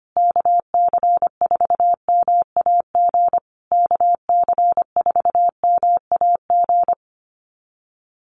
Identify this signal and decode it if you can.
Morse code, heard as a keyed tone, about 25 wpm, 700 Hz. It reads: KC4MAG KC4MAG
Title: USA Call Signs Repeated 3x 25wpm